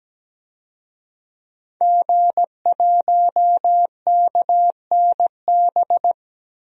Morse code, heard as a keyed tone, about 17 wpm, 700 Hz. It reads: G1KNB
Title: DXCC-100 Call Signs - 5 Characters - Repeated 2x - 17wpm